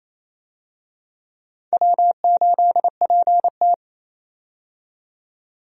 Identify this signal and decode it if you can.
Morse code, heard as a keyed tone, about 28 wpm, 700 Hz. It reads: W8PT
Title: USA Call Signs - 4 Characters - 28wpm